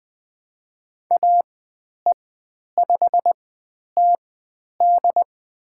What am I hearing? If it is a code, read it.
Morse code, heard as a keyed tone, about 20 wpm, spaced as for 10 wpm, 700 Hz. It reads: AE5TD
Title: 109 Lesson 18g - Letter m - Call Signs 20wpm